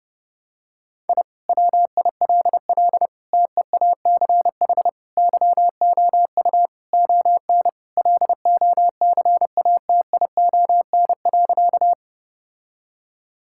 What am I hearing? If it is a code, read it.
Morse code, heard as a keyed tone, about 32 wpm, 700 Hz. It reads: I WILL TEACH YOU ON LOCATION.
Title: Sentences from Top 1000 Words - Review Part 6 - 30wpm